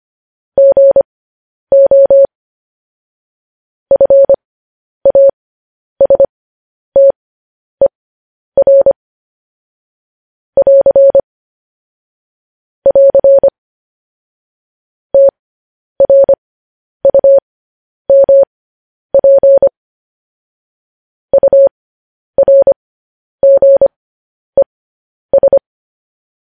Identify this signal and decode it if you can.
Morse code, heard as a keyed tone, about 25 wpm, spaced as for 10 wpm, 550 Hz. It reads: GO FASTER <AR> <AR> TRUMP URGES
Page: news headlines in morse code at 10 wpm